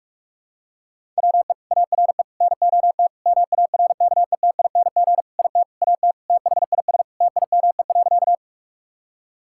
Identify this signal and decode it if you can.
Morse code, heard as a keyed tone, about 46 wpm, 700 Hz. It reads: WE ARE NOT MARKETING IT AT THIS TIME.
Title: Sentences from Top 1000 Words - Review Part 5 - 45wpm